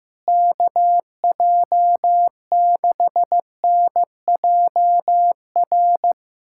Morse code, heard as a keyed tone, about 15 wpm, 700 Hz. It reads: KJ6NJR
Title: USA Callsign DE Callsign - 6 Characters - 15wpm